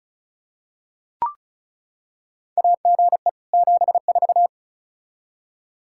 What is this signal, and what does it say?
Morse code, heard as a keyed tone, about 36 wpm, 700 Hz. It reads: AGE 74
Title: Practice QSO - Exchange 3 - More Info - 35wpm